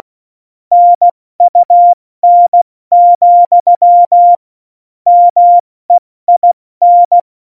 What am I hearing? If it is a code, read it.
Morse code, heard as a keyed tone, about 16 wpm, 700 Hz. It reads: NUN, MEIN
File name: Morsekurs / Witze / Witz 26. Januar 2021, 70 F80